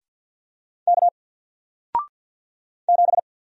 Morse code, heard as a keyed tone, about 48 wpm, 700 Hz. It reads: K 7
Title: Single Character (Letter:Number:Punctuation:Prosign) Alt 50wpm